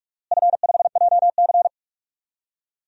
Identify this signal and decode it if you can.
Morse code, heard as a keyed tone, about 46 wpm, 700 Hz. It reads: F5JC
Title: DXCC-100 Call Signs - 4 Characters - Repeated 3x - 45wpm